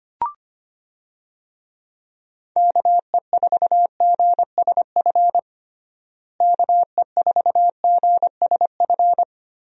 Morse code, heard as a keyed tone, about 25 wpm, 700 Hz. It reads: KE4GSF KE4GSF
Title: USA Call Signs - 6 Characters - Repeated 2x - 25wpm